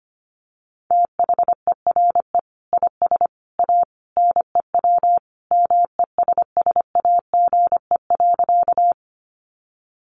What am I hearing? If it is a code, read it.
Morse code, heard as a keyed tone, about 25 wpm, 700 Hz. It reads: THERE IS A NEW MESSAGE.
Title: Sentences from Top 1100 Words 25wpm